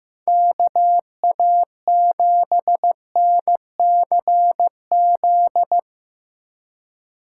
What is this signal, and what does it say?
Morse code, heard as a keyed tone, about 15 wpm, 700 Hz. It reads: KA7NCZ
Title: Speed Racing - USA 2x3 Call Signs 15wpm